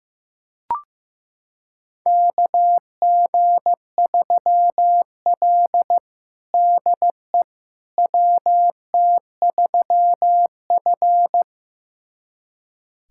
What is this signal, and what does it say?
Morse code, heard as a keyed tone, about 16 wpm, 700 Hz. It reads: KG3L DE WT3F
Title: USA Callsign DE Callsign - 4 Characters - 15wpm